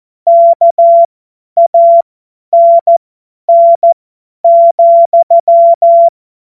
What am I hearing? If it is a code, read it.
Morse code, heard as a keyed tone, about 14 wpm, spaced as for 10 wpm, 680 Hz. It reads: KANN,
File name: Morsekurs / Witze / Witz 04. Januar 2022, 50 F70